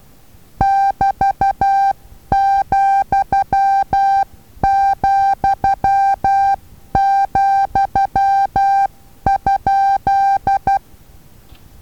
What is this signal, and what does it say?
Morse code, heard as a keyed tone, about 12 wpm, 790 Hz. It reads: <BT>,,,?
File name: Taviro HA6VA A